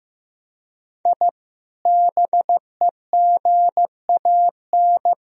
Morse code, heard as a keyed tone, about 16 wpm, 700 Hz. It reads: I BEGAN
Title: Sentences from Top 1000 Words - Review Part 4 - 15wpm